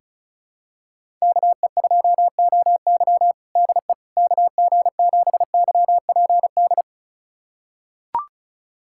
Morse code, heard as a keyed tone, about 34 wpm, 700 Hz. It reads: KE2OY DE KG7YPD
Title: USA Callsign DE Callsign 35wpm